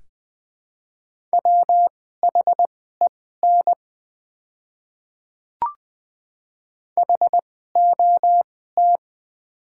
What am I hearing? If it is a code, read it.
Morse code, heard as a keyed tone, about 20 wpm, spaced as for 14 wpm, 700 Hz. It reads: WHEN HOT
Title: 2x Character Spacing - Top 100 Words - 20wpm